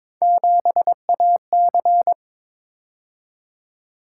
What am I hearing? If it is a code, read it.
Morse code, heard as a keyed tone, about 23 wpm, 700 Hz. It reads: 7AC
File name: practice / 22wpm / Three Letter-Number Combinations - 22wpm